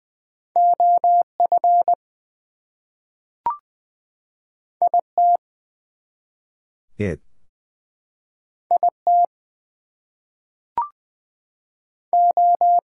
Morse code, heard as a keyed tone, about 20 wpm, 700 Hz. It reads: OF IT IT O
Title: Two Letter Words - 20wpm